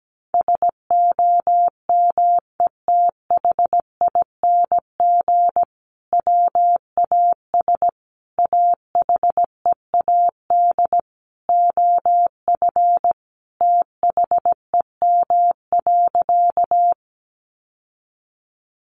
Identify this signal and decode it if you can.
Morse code, heard as a keyed tone, about 17 wpm, 700 Hz. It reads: SOMETHING WAS AHEAD OF THEM.
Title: Sentences from Top 1100 Words 17wpm